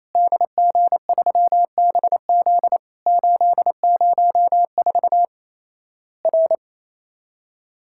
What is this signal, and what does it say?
Morse code, heard as a keyed tone, about 28 wpm, 700 Hz. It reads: DG3BZ 804 R
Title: Worked All Europe DX Contest - QTC Exchange - 28wpm